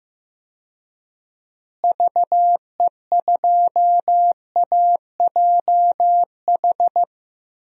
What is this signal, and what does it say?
Morse code, heard as a keyed tone, about 16 wpm, 700 Hz. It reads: VE2AJH
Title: DXCC-100 Call Signs - 6 Characters - Repeated 2x - 15wpm